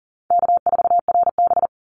Morse code, heard as a keyed tone, about 40 wpm, 700 Hz. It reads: K4RB
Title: Easy Call Signs - 40wpm